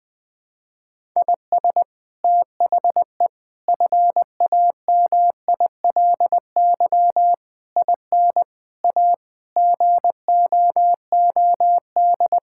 Morse code, heard as a keyed tone, about 20 wpm, 700 Hz. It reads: IS THE FAMILY IN A GOOD
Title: Sentences from Top 1000 Words - Review Part 5 - 20wpm